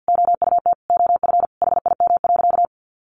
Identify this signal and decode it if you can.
Morse code, heard as a keyed tone, about 50 wpm, 700 Hz. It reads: OUT OF HIM.
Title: Sentences from Top 1500 Words 50wpm